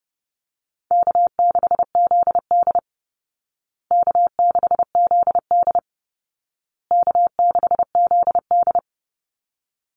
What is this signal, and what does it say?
Morse code, heard as a keyed tone, about 30 wpm, 700 Hz. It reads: K6ZD K6ZD K6ZD
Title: Call Signs Repeated 2x - 30wpm